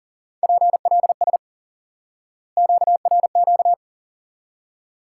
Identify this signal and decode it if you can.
Morse code, heard as a keyed tone, about 40 wpm, 700 Hz. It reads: PLS QRQ
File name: IQE Course 3 - Lesson 89 - YAGI - Two Elements - 40wpm